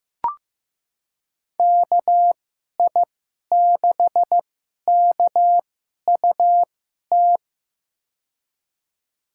Morse code, heard as a keyed tone, about 15 wpm, spaced as for 11 wpm, 700 Hz. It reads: KI6KUT
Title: USA Call Signs - 6 Characters - 2x Character Spacing - 15wpm